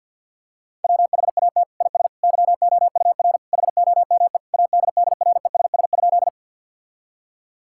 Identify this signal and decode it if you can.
Morse code, heard as a keyed tone, about 50 wpm, 700 Hz. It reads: WHAT IS YOUR HOME ADDRESS?
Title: Sentences from Top 1000 Words - Review Part 6 - 50wpm